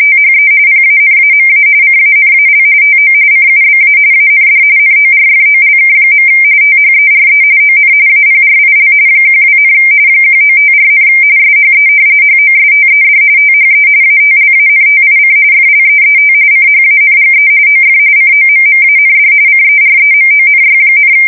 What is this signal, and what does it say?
RTTY, about 45.45 baud. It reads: THE PANAMA CANAL TO CUBA AND NORFOLK. THERE ON APRIL 12, 1954, SHE HOSTED THE ATLANTIC FLEET CHANGE OF COMMAND CEREMONY,
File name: fsk170 / battleshipnj